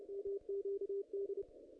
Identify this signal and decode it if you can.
Morse code, heard as a keyed tone, about 31 wpm, 440 Hz. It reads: MQD
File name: contests / ss-cw-k5tr-op-n5rz -2300